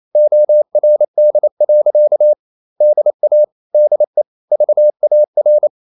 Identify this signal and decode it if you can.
Morse code, heard as a keyed tone, about 28 wpm, 600 Hz. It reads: ORD. DA DE VAR
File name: Ferie QRQ 28wpm